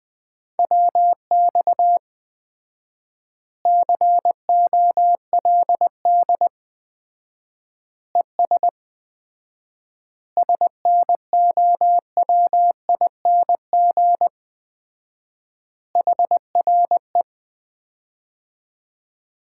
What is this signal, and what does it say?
Morse code, heard as a keyed tone, about 20 wpm, 700 Hz. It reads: WX COLD ES SNOWING HRE
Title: Common QSO Phrases - 4x Word-Spacing - 20wpm